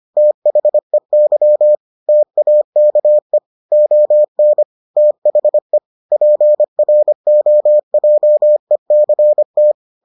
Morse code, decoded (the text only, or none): THEY TAKE ON THE PROJECT